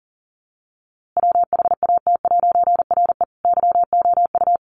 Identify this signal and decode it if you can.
Morse code, heard as a keyed tone, about 40 wpm, 700 Hz. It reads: WHAT'RE YOU
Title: A Sound Of Thunder 40wpm